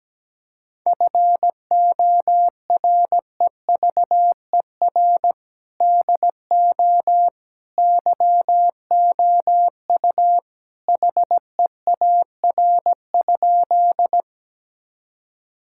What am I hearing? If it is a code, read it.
Morse code, heard as a keyed tone, about 17 wpm, 700 Hz. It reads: FOREVER DO YOU HEAR?
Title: Sentences from Top 1600 Words 17wpm